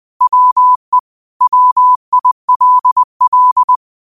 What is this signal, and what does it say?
Morse code, heard as a keyed tone, about 20 wpm, 1000 Hz. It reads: WE WILL